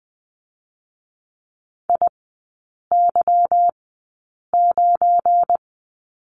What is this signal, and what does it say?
Morse code, heard as a keyed tone, about 20 wpm, spaced as for 8 wpm, 700 Hz. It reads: IY9
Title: 184 Lesson 30e - Letter b - Three Characters 20wpm